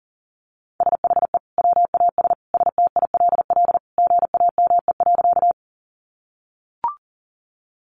Morse code, heard as a keyed tone, about 40 wpm, 700 Hz. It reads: SHE WAS STILL GAME.
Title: Sentences from Top 500 Words 40wpm